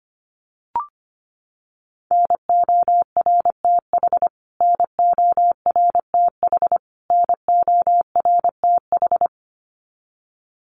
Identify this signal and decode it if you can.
Morse code, heard as a keyed tone, about 24 wpm, 700 Hz. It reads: NORTH NORTH NORTH
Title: Top 300 Words Repeated 25wpm